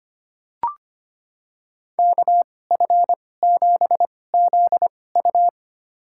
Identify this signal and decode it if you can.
Morse code, heard as a keyed tone, about 25 wpm, spaced as for 18 wpm, 700 Hz. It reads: KF7ZU
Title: USA Call Signs - 5 Characters - 2x Character Spacing - 25wpm